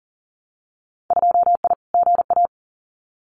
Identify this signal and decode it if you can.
Morse code, heard as a keyed tone, about 39 wpm, 700 Hz. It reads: 2I GA
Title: Winter Field Day Exchanges 40wpm